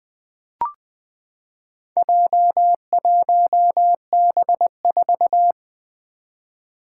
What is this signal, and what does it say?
Morse code, heard as a keyed tone, about 20 wpm, 700 Hz. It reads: J1B4
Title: Four Challenging Letter-Number Combinations 20wpm